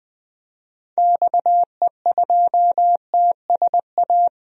Morse code, heard as a keyed tone, about 20 wpm, 700 Hz. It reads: XE2TSA
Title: DXCC-100 Call Signs - 6 Characters - Repeated 2x - 20wpm